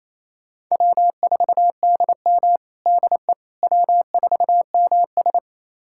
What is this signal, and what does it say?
Morse code, heard as a keyed tone, about 27 wpm, 700 Hz. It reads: W4DM DE W4MS
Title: USA Callsign DE Callsign - 4 Characters - 28wpm